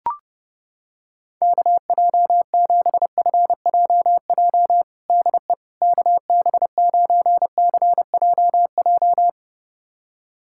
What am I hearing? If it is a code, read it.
Morse code, heard as a keyed tone, about 28 wpm, 700 Hz. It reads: KJ7FJJ DE KB9CJJ
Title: USA Callsign DE Callsign - 6 Characters - 30wpm